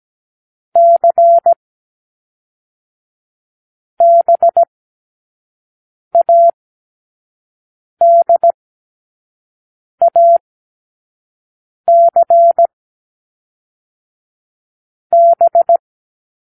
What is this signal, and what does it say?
Morse code, heard as a keyed tone, about 17 wpm, 680 Hz. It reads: C B A D A C B
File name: k7qo / Lesson6(A-D)